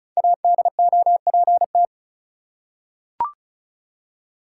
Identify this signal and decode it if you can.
Morse code, heard as a keyed tone, about 36 wpm, 700 Hz. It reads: ADOPT
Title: Five Letter Words - 35wpm